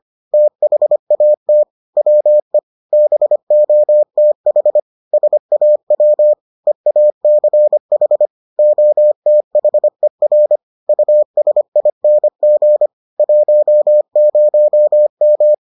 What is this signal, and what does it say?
Morse code, heard as a keyed tone, about 25 wpm, 600 Hz. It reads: THAT WE BOTH SAW EACH OTHER USING 10M